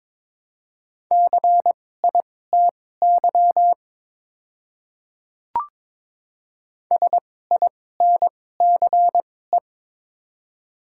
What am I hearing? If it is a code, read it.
Morse code, heard as a keyed tone, about 22 wpm, spaced as for 16 wpm, 700 Hz. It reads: CITY SINCE
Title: 2x Character Spacing - Top 300 Words - 22wpm